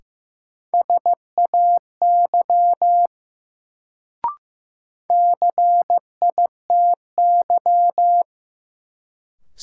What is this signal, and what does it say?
Morse code, heard as a keyed tone, about 15 wpm, 700 Hz. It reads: SAY CITY
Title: Sets of 2 Words - New Word Drill - 15wpm